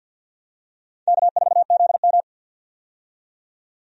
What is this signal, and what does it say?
Morse code, heard as a keyed tone, about 50 wpm, 700 Hz. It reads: K4ZM
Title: USA Call Signs - 4 Characters - 50wpm